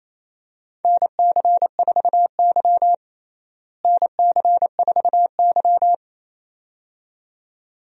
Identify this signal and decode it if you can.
Morse code, heard as a keyed tone, about 28 wpm, 700 Hz. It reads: NC4Y NC4Y
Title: USA Call Signs - 4 Characters - Repeated 2x - 28wpm